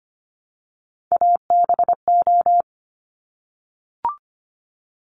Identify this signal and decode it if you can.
Morse code, heard as a keyed tone, about 25 wpm, 700 Hz. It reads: ABO
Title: Top Three Letter Combinations in Words 25wpm